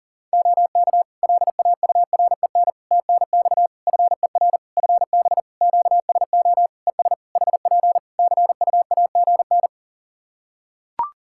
Morse code, heard as a keyed tone, about 39 wpm, 700 Hz. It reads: OK LAUREN TNX FER FB QSO ES HP CUAGN
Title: Practice QSO - Exchange 4 - Ending - 40wpm